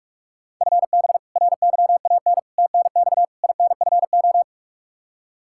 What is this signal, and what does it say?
Morse code, heard as a keyed tone, about 44 wpm, 700 Hz. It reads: FB RYAN TNX INFO
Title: K1USN SST QSO Practice 45wpm